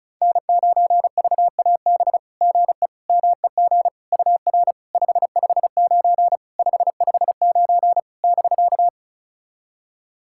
Morse code, heard as a keyed tone, about 36 wpm, 690 Hz. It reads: N9VAB GE MEG UR 559 559 <BK>
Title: SOTA QSO Practice 35wpm